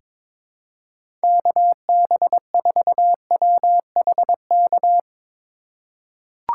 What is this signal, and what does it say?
Morse code, heard as a keyed tone, about 21 wpm, 700 Hz. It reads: KB4WHK
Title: USA Call Signs - 6 Characters - 3x Character Spacing - 22wpm